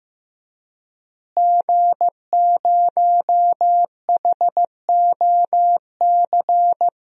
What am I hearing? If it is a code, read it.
Morse code, heard as a keyed tone, about 15 wpm, 700 Hz. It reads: G0HOC
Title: DXCC-100 Call Signs - 5 Characters - Repeated 3x - 15wpm